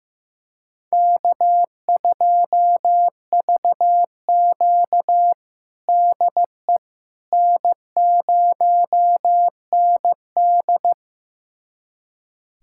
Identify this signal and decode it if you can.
Morse code, heard as a keyed tone, about 15 wpm, 700 Hz. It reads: K2VQ DE N0ND
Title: USA Callsign DE Callsign - 4 Characters - 15wpm